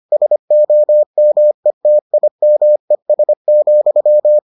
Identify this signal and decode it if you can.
Morse code, heard as a keyed tone, about 25 wpm, 600 Hz. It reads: SOMETIMES,